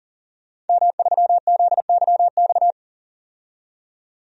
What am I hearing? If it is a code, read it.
Morse code, heard as a keyed tone, about 40 wpm, 700 Hz. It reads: M3ZYX
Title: DXCC-100 Call Signs - 5 Characters - 40wpm